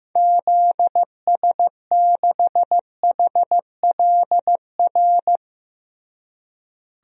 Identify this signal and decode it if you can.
Morse code, heard as a keyed tone, about 15 wpm, 700 Hz. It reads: ZS6HLR
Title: DXCC-100 Call Signs - 6 Characters - Repeated 3x - 15wpm